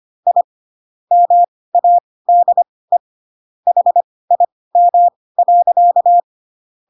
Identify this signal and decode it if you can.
Morse code, heard as a keyed tone, about 25 wpm, spaced as for 18 wpm, 700 Hz. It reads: I MADE HIM.
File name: Pinocchio ch3 12 25 ik1hge